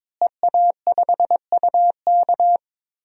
Morse code, heard as a keyed tone, about 21 wpm, 700 Hz. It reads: EA5UK
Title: DXCC-100 Call Signs - 5 Characters - Repeated 3x - 22wpm